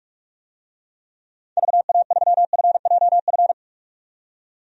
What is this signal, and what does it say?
Morse code, heard as a keyed tone, about 46 wpm, 700 Hz. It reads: VA3FJF
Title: DXCC-100 Call Signs - 6 Characters - 45wpm